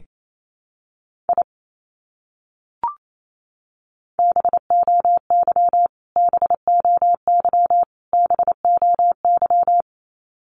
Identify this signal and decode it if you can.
Morse code, heard as a keyed tone, about 28 wpm, 700 Hz. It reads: I BOY BOY BOY
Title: Top 200 Words Repeated 28wpm